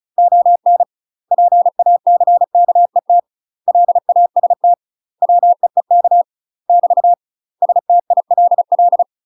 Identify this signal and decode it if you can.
Morse code, heard as a keyed tone, about 34 wpm, 700 Hz. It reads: ON PACKET LAST WEEK <BT> STILL
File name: UK MASTER QSOs TRAINING / 35WPM / UK MASTER QSOs 0042